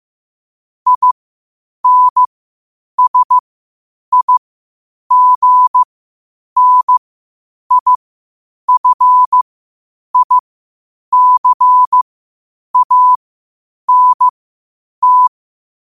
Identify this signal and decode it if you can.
Morse code, heard as a keyed tone, about 15 wpm, spaced as for 8 wpm, 1000 Hz. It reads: INSIGNIFICANT
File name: qotdfiles / 8wpm